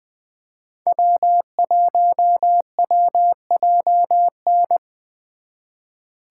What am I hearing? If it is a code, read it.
Morse code, heard as a keyed tone, about 25 wpm, 700 Hz. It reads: W1WJN
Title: Speed Racing - USA 1x3 Call Signs 20wpm